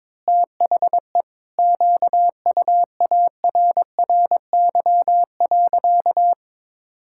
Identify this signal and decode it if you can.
Morse code, heard as a keyed tone, about 21 wpm, 700 Hz. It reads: THE QUARRY.
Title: Pangram Challenge v2 - 22wpm